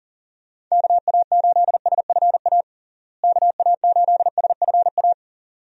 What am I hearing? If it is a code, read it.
Morse code, heard as a keyed tone, about 40 wpm, 700 Hz. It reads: KA8SFA KA8SFA
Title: USA Call Signs - 6 Characters - Repeated 2x - 40wpm